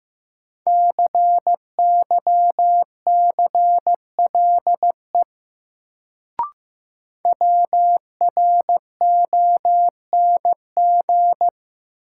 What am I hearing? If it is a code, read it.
Morse code, heard as a keyed tone, about 15 wpm, 700 Hz. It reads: CYCLE WRONG
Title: Sentences from Top 900 Words - New Word Drill - 15wpm